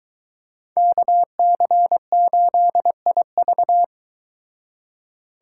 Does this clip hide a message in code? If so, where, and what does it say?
Yes: Morse code from 0.8 to 3.8 s, heard as a keyed tone, about 22 wpm, 700 Hz. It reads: KC8IV